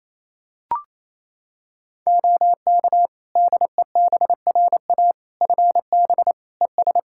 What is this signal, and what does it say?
Morse code, heard as a keyed tone, about 28 wpm, 700 Hz. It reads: OK DEBRA FB ES
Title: Practice QSO - Exchange 2 - Info - 28wpm